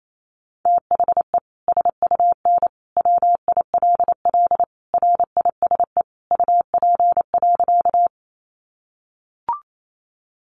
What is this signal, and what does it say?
Morse code, heard as a keyed tone, about 28 wpm, 700 Hz. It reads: THE SUN WILL RISE UP.
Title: Sentences from Top 900 Words 28wpm